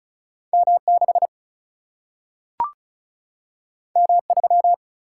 Morse code, heard as a keyed tone, about 36 wpm, 700 Hz. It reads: M 6 M 3
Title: Single-Letter Single-Number 35wpm